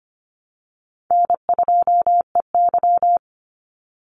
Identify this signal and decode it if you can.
Morse code, heard as a keyed tone, about 25 wpm, 700 Hz. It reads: N2EY
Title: Easy Call Signs - 25wpm